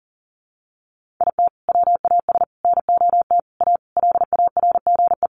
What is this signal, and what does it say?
Morse code, heard as a keyed tone, about 40 wpm, 700 Hz. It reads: IT WAS NOT A LARGE
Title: Sentences from Top 1400 Words 40wpm